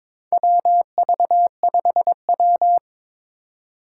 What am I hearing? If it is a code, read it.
Morse code, heard as a keyed tone, about 22 wpm, 700 Hz. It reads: WV5W
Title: USA Call Signs - 4 Characters - 4x Character Spacing - 22wpm